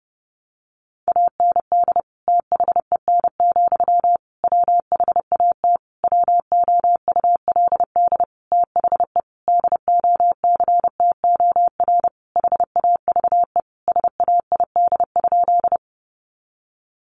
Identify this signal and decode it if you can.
Morse code, heard as a keyed tone, about 30 wpm, 700 Hz. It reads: AND THEN, WHAT WOULD THE DOCTOR HAVE SAID?
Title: Sentences from Top 1500 Words 30wpm